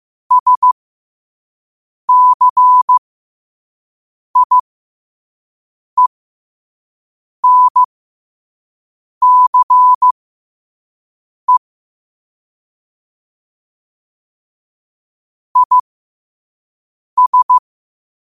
Morse code, heard as a keyed tone, about 15 wpm, spaced as for 5 wpm, 1000 Hz. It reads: SCIENCE IS
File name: qotdfiles / 5wpm